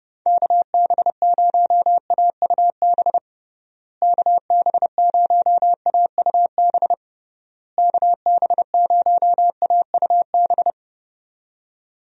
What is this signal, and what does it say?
Morse code, heard as a keyed tone, about 30 wpm, 700 Hz. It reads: KB0AUB KB0AUB KB0AUB
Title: USA Call Signs - 6 Characters - Repeated 3x - 30wpm